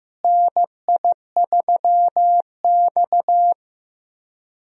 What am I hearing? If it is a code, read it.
Morse code, heard as a keyed tone, about 15 wpm, 700 Hz. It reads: NI3X
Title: USA Call Signs - 4 Characters - Repeated 2x - 15wpm